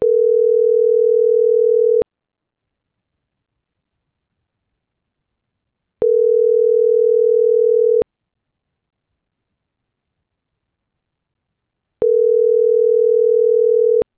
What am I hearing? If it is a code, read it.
Ringback tone (North America), 440 + 480 Hz.
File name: CENTRAL GENERAL / out deenis.chavez